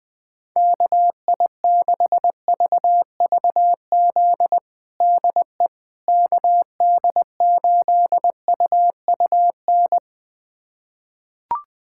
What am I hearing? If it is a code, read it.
Morse code, heard as a keyed tone, about 21 wpm, 700 Hz. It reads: KI6VVZ DE KD8UUN
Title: USA Callsign DE Callsign - 6 Characters - 20wpm